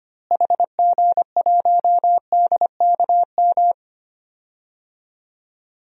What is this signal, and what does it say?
Morse code, heard as a keyed tone, about 25 wpm, 700 Hz. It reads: HG1DKM
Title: DXCC-100 Call Signs - 6 Characters - Repeated 3x - 25wpm